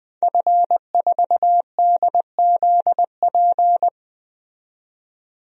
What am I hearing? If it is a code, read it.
Morse code, heard as a keyed tone, about 20 wpm, 700 Hz. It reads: F4DZP
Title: DXCC-100 Call Signs - 5 Characters - 20wpm